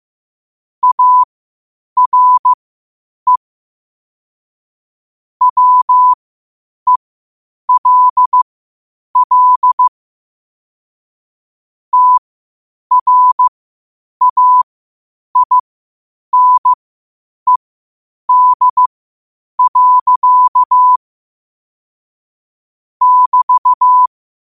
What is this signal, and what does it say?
Morse code, heard as a keyed tone, about 15 wpm, spaced as for 8 wpm, 1000 Hz. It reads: ARE WELL TRAINED. <BT>
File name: qotdfiles / 8wpm